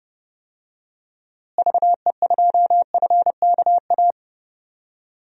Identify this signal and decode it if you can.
Morse code, heard as a keyed tone, about 30 wpm, 700 Hz. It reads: VE2FKA
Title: DXCC-100 Call Signs - 6 Characters - Repeated 3x - 30wpm